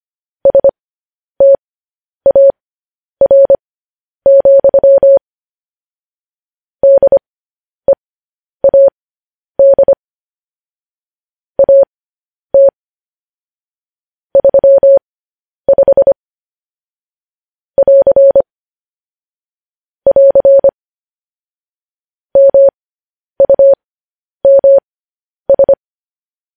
Morse code, heard as a keyed tone, about 25 wpm, spaced as for 10 wpm, 550 Hz. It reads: STAR, DEAD AT 35 <AR> <AR> MUMS
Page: news-headlines-in-morse-code-at-10-wpm